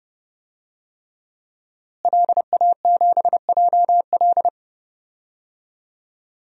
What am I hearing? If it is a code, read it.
Morse code, heard as a keyed tone, about 30 wpm, 700 Hz. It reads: LA7JL
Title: DXCC-100 Call Signs - 5 Characters - Repeated 3x - 30wpm